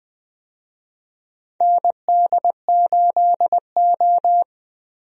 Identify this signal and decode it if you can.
Morse code, heard as a keyed tone, about 20 wpm, 700 Hz. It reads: ND8O
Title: USA Call Signs - 4 Characters - 20wpm